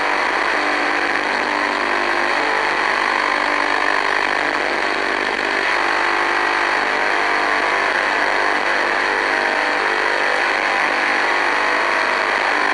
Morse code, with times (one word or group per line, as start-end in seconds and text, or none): none